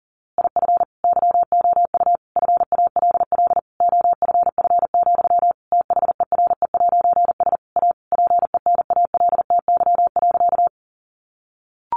Morse code, heard as a keyed tone, about 40 wpm, 700 Hz. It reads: IF YOU FALL OFF, THERE'S A PENALTY.
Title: A Sound Of Thunder 40wpm